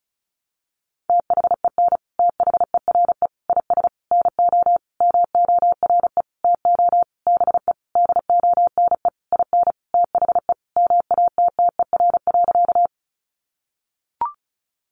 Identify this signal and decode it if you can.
Morse code, heard as a keyed tone, about 34 wpm, 700 Hz. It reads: THEN THERE IS NO MORE TO BE DONE IN THE MATTER.
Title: Sentences from Top 700 Words 35wpm